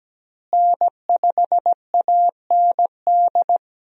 Morse code, heard as a keyed tone, about 17 wpm, 700 Hz. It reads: N5AND
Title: USA Call Signs - 5 Characters - Repeated 2x - 17wpm